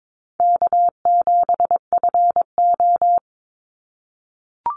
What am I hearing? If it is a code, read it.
Morse code, heard as a keyed tone, about 22 wpm, 700 Hz. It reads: K7FO
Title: NA SOTA Chaser Callsigns 22wpm